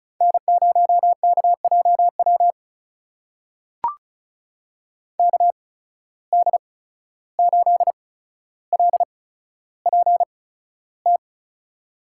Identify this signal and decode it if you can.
Morse code, heard as a keyed tone, about 36 wpm, 700 Hz. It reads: N 0 K J W K D 8 L P T
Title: USA Call Signs - 8x Character Spacing - 35wpm